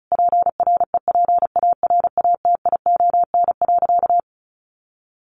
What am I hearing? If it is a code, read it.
Morse code, heard as a keyed tone, about 34 wpm, 700 Hz. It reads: PREPARATION.
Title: Sentences from Top 1400 Words 35wpm